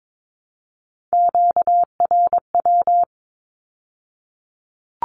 Morse code, heard as a keyed tone, about 22 wpm, 700 Hz. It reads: QRW
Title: Q-Codes with Spoken Meaning - 22wpm